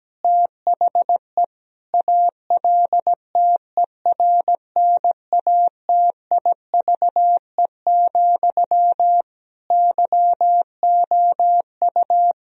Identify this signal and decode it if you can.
Morse code, heard as a keyed tone, about 17 wpm, 700 Hz. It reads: THE ALTERNATIVE, YOU
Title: Sentences from Top 1000 Words - Review Part 4 - 17wpm